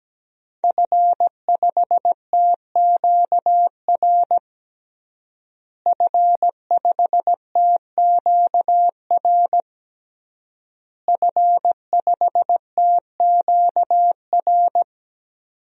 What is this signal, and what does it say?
Morse code, heard as a keyed tone, about 17 wpm, 700 Hz. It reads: F5TQR F5TQR F5TQR
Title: DXCC-100 Call Signs - 5 Characters - Repeated 3x - 17wpm